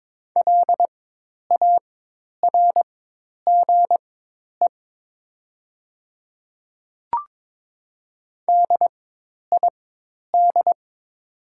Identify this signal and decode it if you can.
Morse code, heard as a keyed tone, about 22 wpm, spaced as for 10 wpm, 690 Hz. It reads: LARGE DID
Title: 4x Character Spacing - Top 200 Words - 22wpm